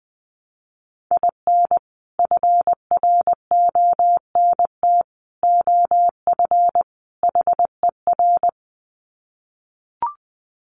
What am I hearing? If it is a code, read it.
Morse code, heard as a keyed tone, about 20 wpm, 700 Hz. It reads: IN FRONT OF HER
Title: Sets of 4 Words 20wpm